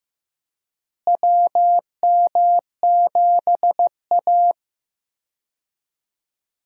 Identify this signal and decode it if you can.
Morse code, heard as a keyed tone, about 15 wpm, 700 Hz. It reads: WM7A
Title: USA Call Signs - 4 Characters - Repeated 3x - 15wpm